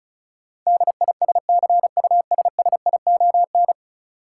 Morse code, heard as a keyed tone, about 34 wpm, 700 Hz. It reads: DISCUSSION
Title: Sentences from Top 1100 Words - New Word Drill - 35wpm